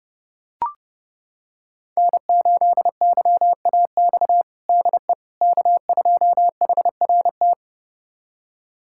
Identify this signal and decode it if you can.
Morse code, heard as a keyed tone, about 28 wpm, 700 Hz. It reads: N8YAX DE K2HRT
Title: USA Callsign DE Callsign - 5 Characters - 30wpm